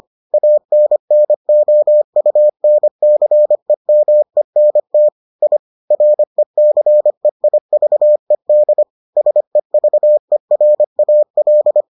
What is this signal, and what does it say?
Morse code, heard as a keyed tone, about 25 wpm, 600 Hz. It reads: ANNOUNCEMENT I RECEIVED SEVERAL